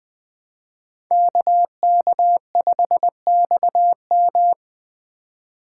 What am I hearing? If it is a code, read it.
Morse code, heard as a keyed tone, about 20 wpm, 700 Hz. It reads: KK5XM